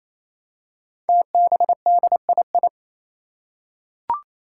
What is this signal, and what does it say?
Morse code, heard as a keyed tone, about 27 wpm, 700 Hz. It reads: TBDII
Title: Five Challenging Letter-Number Combinations 28wpm